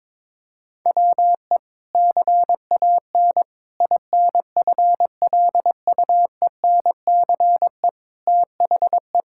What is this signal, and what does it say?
Morse code, heard as a keyed tone, about 22 wpm, 700 Hz. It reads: WE CAN INFLUENCE THE
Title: Sentences from Top 1600 Words 22wpm